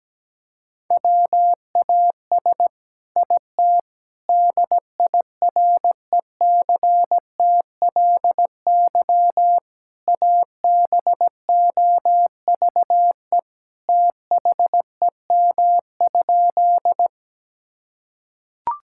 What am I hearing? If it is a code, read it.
Morse code, heard as a keyed tone, about 17 wpm, 700 Hz. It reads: WAS IT DIRECTLY ABOVE THEM?
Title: Sentences from Top 1000 Words - Review Part 5 - 17wpm